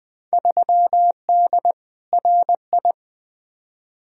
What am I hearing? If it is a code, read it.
Morse code, heard as a keyed tone, about 20 wpm, 700 Hz. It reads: 3D RI
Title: Field Day Exchanges Encore - 20wpm